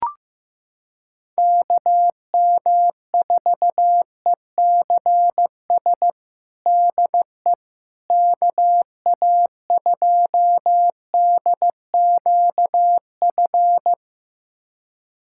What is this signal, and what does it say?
Morse code, heard as a keyed tone, about 15 wpm, 700 Hz. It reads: KM4ECS DE KA2DQF
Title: USA Callsign DE Callsign - 6 Characters - 15wpm